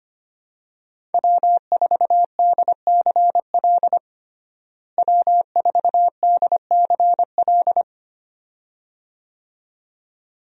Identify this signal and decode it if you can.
Morse code, heard as a keyed tone, about 25 wpm, 700 Hz. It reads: W4DCL W4DCL
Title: USA Call Signs - 5 Characters - Repeated 2x - 25wpm